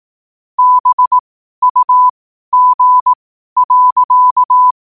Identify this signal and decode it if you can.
Morse code, heard as a keyed tone, about 18 wpm, spaced as for 12 wpm, 1000 Hz. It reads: BUG.
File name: qotdfiles / 12wpm